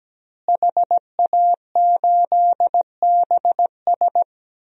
Morse code, heard as a keyed tone, about 17 wpm, 700 Hz. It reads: HA8BS
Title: DXCC-100 Call Signs - 5 Characters - Repeated 2x - 17wpm